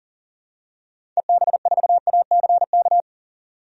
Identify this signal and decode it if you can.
Morse code, heard as a keyed tone, about 37 wpm, 700 Hz. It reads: EB4ACK
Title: DXCC-100 Call Signs - 6 Characters - 40wpm